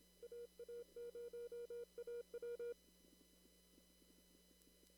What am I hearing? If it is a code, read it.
Morse code, heard as a keyed tone, about 27 wpm, 480 Hz. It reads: AA0AW
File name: CW / kt5j wpx cw-op-k5pi